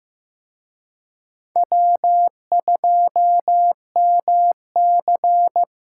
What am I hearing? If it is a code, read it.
Morse code, heard as a keyed tone, about 18 wpm, 700 Hz. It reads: W2MC
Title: Speed Racing - USA 1x2 Call Signs 15wpm